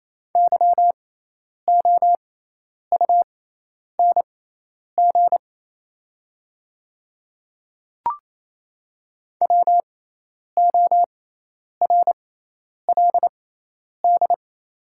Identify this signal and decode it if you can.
Morse code, heard as a keyed tone, about 28 wpm, spaced as for 10 wpm, 700 Hz. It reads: YOUNG WORLD
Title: 6x Character Spacing - Top 500 Words - 28wpm